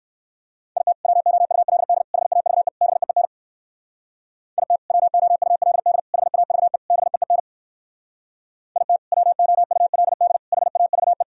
Morse code, heard as a keyed tone, about 58 wpm, 700 Hz. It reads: IT WOULD HAVE BEEN IT WOULD HAVE BEEN IT WOULD HAVE